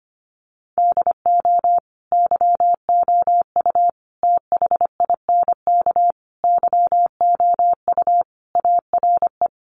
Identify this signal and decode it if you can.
Morse code, heard as a keyed tone, about 25 wpm, 700 Hz. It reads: DO YOU THINK YOU ARE
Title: Sentences from Top 1200 Words 25wpm